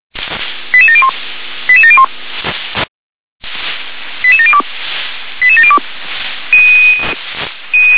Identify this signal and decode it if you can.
Five-tone selective call (ZVEI), heard as quick runs of five tones, tones of about 70 ms. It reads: 88701 88701 88702 88702
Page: SDR/ZVEI